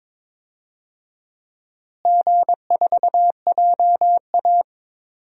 Morse code, heard as a keyed tone, about 22 wpm, 700 Hz. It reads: G4JA
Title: DXCC-100 Call Signs - 4 Characters - Repeated 2x - 22wpm